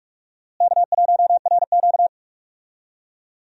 Morse code, heard as a keyed tone, about 46 wpm, 700 Hz. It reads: K1RQ
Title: USA Call Signs - 4 Characters - Repeated 2x - 45wpm